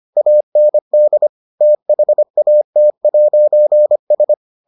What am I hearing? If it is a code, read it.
Morse code, heard as a keyed tone, about 25 wpm, 600 Hz. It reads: AND THAT'S